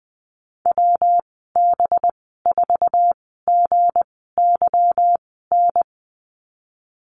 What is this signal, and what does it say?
Morse code, heard as a keyed tone, about 20 wpm, spaced as for 14 wpm, 700 Hz. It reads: WB4GYN
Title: 193 Lesson 31g - Letter v - Call Signs 20wpm